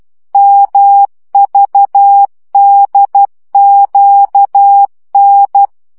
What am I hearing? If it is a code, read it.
Morse code, heard as a keyed tone, about 12 wpm, 800 Hz. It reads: MVDQN